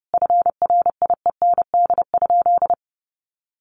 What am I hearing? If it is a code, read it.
Morse code, heard as a keyed tone, about 28 wpm, 700 Hz. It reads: FRIEND?
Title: Sentences from Top 1400 Words 30wpm